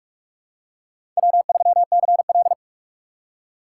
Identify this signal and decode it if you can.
Morse code, heard as a keyed tone, about 44 wpm, 690 Hz. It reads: W3CL
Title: USA Call Signs - 4 Characters - 3x Character Spacing - 45wpm